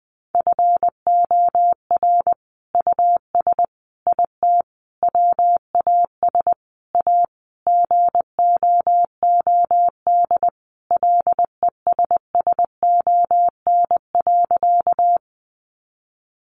Morse code, heard as a keyed tone, about 20 wpm, 700 Hz. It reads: FOR US IT WAS A GOOD LESSON.
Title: Sentences from Top 1200 Words 20wpm